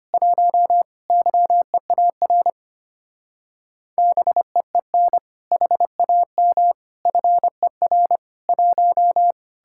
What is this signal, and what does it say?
Morse code, heard as a keyed tone, about 33 wpm, 700 Hz. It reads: 1 YEAR BEEN HAM FER 1